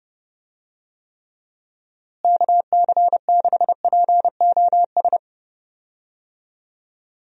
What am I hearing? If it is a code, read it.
Morse code, heard as a keyed tone, about 30 wpm, 700 Hz. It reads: KC6POS
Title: USA Call Signs - 6 Characters - Repeated 2x - 30wpm